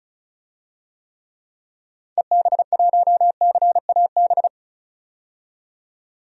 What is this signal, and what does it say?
Morse code, heard as a keyed tone, about 36 wpm, 690 Hz. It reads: EB1CAB
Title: DXCC-100 Call Signs - 6 Characters - 35wpm